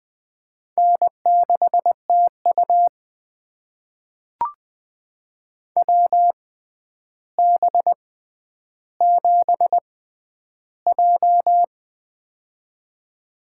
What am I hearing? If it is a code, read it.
Morse code, heard as a keyed tone, about 20 wpm, spaced as for 10 wpm, 700 Hz. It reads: N6TU WB7J
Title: USA Call Signs - 4 Characters - 6x Character Spacing - 20wpm